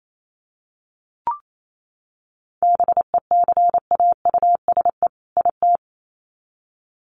Morse code, heard as a keyed tone, about 28 wpm, 700 Hz. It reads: BECAUSE IT
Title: Sets of 2 Words 28wpm